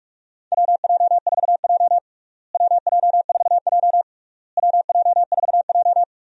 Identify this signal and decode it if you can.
Morse code, heard as a keyed tone, about 41 wpm, 700 Hz. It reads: WJ4J WJ4J WJ4J
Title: USA Call Signs - 4 Characters - Repeated 3x - 45wpm